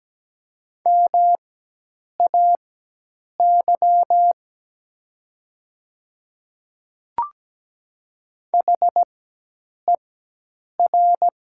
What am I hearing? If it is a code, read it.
Morse code, heard as a keyed tone, about 17 wpm, spaced as for 8 wpm, 700 Hz. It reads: MAY HER
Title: 4x Character Spacing - Top 200 Words - 17wpm